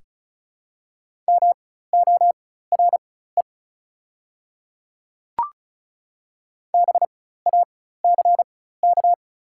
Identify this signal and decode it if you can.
Morse code, heard as a keyed tone, about 36 wpm, spaced as for 16 wpm, 700 Hz. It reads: MORE BACK
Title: 4x Character Spacing - Top 200 Words - 35wpm